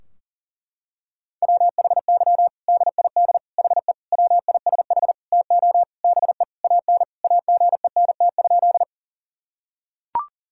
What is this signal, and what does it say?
Morse code, heard as a keyed tone, about 40 wpm, 700 Hz. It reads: WHY DID HE WISH TO BE AN AGENT?
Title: Sentences from Top 1600 Words 40wpm